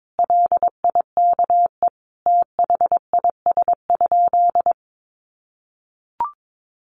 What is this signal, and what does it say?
Morse code, heard as a keyed tone, about 22 wpm, 700 Hz. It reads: LIKE THIS?
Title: Sentences from Top 1300 Words 22wpm